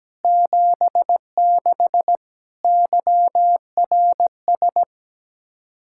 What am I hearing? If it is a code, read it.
Morse code, heard as a keyed tone, about 17 wpm, 700 Hz. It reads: 76 YRS
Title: Practice QSO - Exchange 3 - More Info - 17wpm